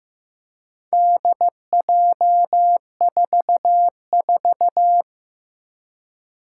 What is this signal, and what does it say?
Morse code, heard as a keyed tone, about 15 wpm, 700 Hz. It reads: DJ44
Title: Four Challenging Letter-Number Combinations 15wpm